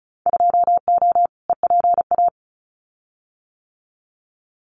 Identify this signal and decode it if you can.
Morse code, heard as a keyed tone, about 34 wpm, 700 Hz. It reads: 2O EPA
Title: Winter Field Day Exchanges Encore 35wpm